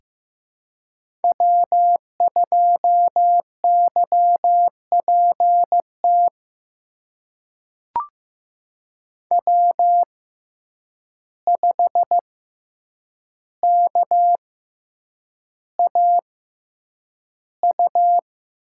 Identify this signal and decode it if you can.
Morse code, heard as a keyed tone, about 15 wpm, spaced as for 8 wpm, 700 Hz. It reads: W2YPT W5KAU
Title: USA Call Signs - 5 Characters - 6x Character Spacing - 15wpm